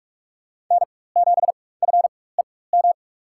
Morse code, heard as a keyed tone, about 47 wpm, spaced as for 21 wpm, 700 Hz. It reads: N7FEM
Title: USA Call Signs - 5 Characters - 4x Character Spacing - 45wpm